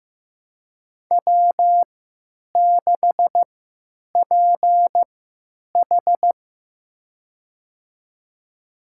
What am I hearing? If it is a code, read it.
Morse code, heard as a keyed tone, about 15 wpm, spaced as for 9 wpm, 700 Hz. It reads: W6PH
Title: USA Call Signs - 4 Characters - 3x Character Spacing - 15wpm